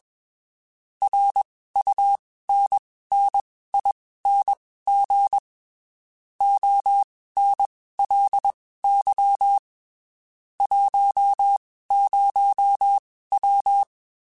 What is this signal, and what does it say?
Morse code, heard as a keyed tone, about 21 wpm, spaced as for 15 wpm, 800 Hz. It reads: RUNNING ONLY 10W